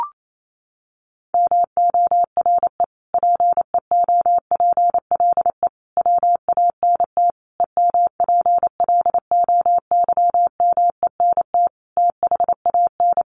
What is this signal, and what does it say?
Morse code, heard as a keyed tone, about 28 wpm, 700 Hz. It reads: MORE PEOPLE WANT EMPLOYMENT THAN
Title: Sentences from Top 1500 Words 28wpm